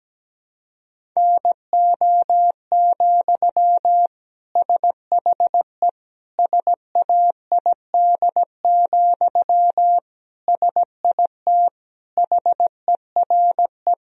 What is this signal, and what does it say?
Morse code, heard as a keyed tone, about 17 wpm, 700 Hz. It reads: NO, SHE SAID, SIT HERE
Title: Sentences from Top 1000 Words - Review Part 1 - 17wpm